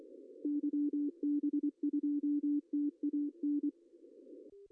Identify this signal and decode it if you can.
Morse code, heard as a keyed tone, about 24 wpm, 290 Hz. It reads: YB2TAN
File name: CW / cqww-cw-k5tr